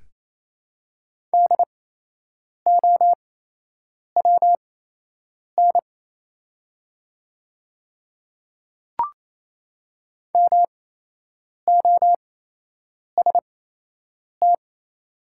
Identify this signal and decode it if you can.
Morse code, heard as a keyed tone, about 28 wpm, spaced as for 8 wpm, 700 Hz. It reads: DOWN MOST
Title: 8x Character Spacing - Top 200 Words - 28wpm